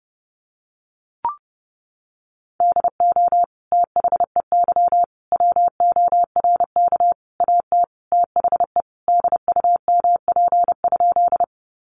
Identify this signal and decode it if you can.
Morse code, heard as a keyed tone, about 30 wpm, 700 Hz. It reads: DO THEY WORK AT THE DUMP?
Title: Sentences from Top 1300 Words 30wpm